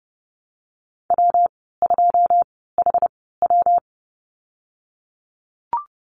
Morse code, heard as a keyed tone, about 30 wpm, spaced as for 17 wpm, 700 Hz. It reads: W2HW
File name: Lesson 19g - Letter w - Call Signs 30wpm